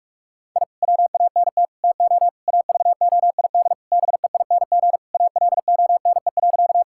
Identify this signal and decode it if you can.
Morse code, heard as a keyed tone, about 46 wpm, 700 Hz. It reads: I WANT TO AVOID BEING ALONE.
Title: Sentences from Top 1000 Words - Review Part 6 - 45wpm